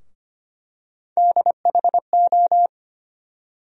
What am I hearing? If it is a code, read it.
Morse code, heard as a keyed tone, about 25 wpm, 700 Hz. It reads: DHO
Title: USA Call Sign Postfixes - 3 Letters 25wpm